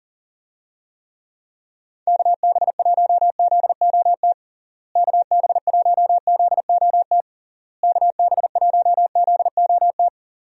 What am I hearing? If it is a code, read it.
Morse code, heard as a keyed tone, about 40 wpm, 700 Hz. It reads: KB1ZOT KB1ZOT KB1ZOT
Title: USA Call Signs - 6 Characters - Repeated 3x - 40wpm